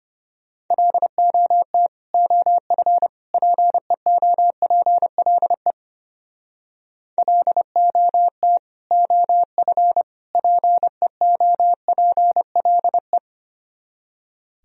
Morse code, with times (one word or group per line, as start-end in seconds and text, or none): 0.7-1.9 LOT
2.1-3.1 OF
3.3-5.7 PEOPLE
7.2-8.6 LOT
8.9-10.0 OF
10.4-13.2 PEOPLE